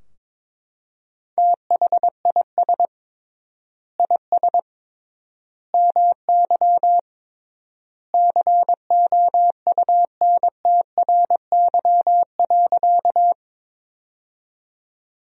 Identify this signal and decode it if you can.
Morse code, heard as a keyed tone, about 22 wpm, 700 Hz. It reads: THIS IS MY COUNTRY.
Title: 3x Word-Spacing - Sentences from Top 300 Words - 22wpm